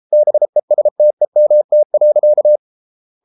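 Morse code, heard as a keyed tone, about 33 wpm, 600 Hz. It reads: BESTEMT.
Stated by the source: Morse code at 33 wpm, QRQ.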